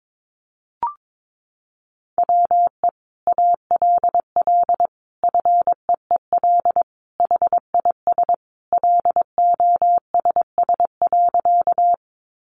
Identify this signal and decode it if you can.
Morse code, heard as a keyed tone, about 22 wpm, 700 Hz. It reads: WE ALL FEEL HIS LOSS.
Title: Sentences from Top 1400 Words 22wpm